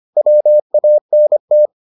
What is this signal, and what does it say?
Morse code, heard as a keyed tone, about 25 wpm, 600 Hz. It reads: WANT